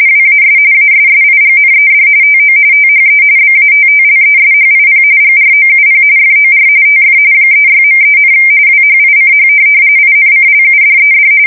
RTTY, about 45.45 baud. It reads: WAS ASSIGNED TO THE BATTLESHIP GROUP FOR TF 36.3, WHICH INCLUDED